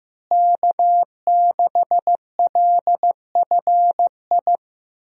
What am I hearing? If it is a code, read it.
Morse code, heard as a keyed tone, about 15 wpm, 700 Hz. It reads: K6LFI
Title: USA Call Signs - 5 Characters - Repeated 3x - 15wpm